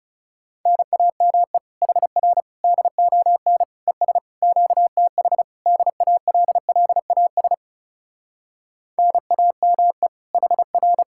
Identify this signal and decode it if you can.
Morse code, heard as a keyed tone, about 40 wpm, 700 Hz. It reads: NAME HR DON ES QTH DALLAS NAME HR